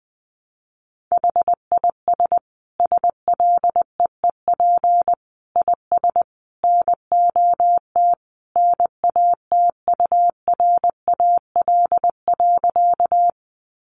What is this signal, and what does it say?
Morse code, heard as a keyed tone, about 20 wpm, 700 Hz. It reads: HIS SLEEP IS NOT NATURAL.
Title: Sentences from Top 700 Words 20wpm